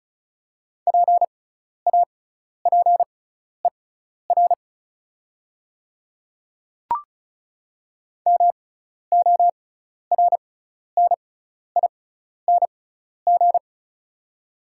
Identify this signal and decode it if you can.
Morse code, heard as a keyed tone, about 36 wpm, spaced as for 12 wpm, 690 Hz. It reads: PAPER MORNING
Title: 6x Character Spacing - Top 500 Words - 35wpm